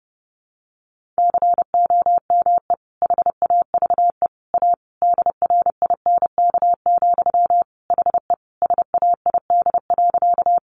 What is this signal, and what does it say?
Morse code, heard as a keyed tone, about 30 wpm, 700 Hz. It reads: COME HAVE A DRINK, HE SAID.
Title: Sentences from Top 1500 Words 30wpm